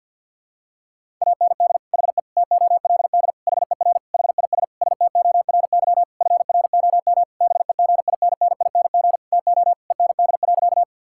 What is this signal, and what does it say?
Morse code, heard as a keyed tone, about 52 wpm, 690 Hz. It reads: AND HE TOLD HER HIS STORY FROM BEGINNING TO END.
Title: Sentences from Top 1000 Words - Review Part 5 - 50wpm